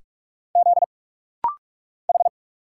Morse code, heard as a keyed tone, about 46 wpm, 700 Hz. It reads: Z H
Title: Single Letters - Rapid-Fire 45wpm